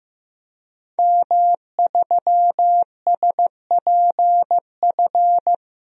Morse code, heard as a keyed tone, about 15 wpm, 700 Hz. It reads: M3SPF
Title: DXCC-100 Call Signs - 5 Characters - Repeated 3x - 15wpm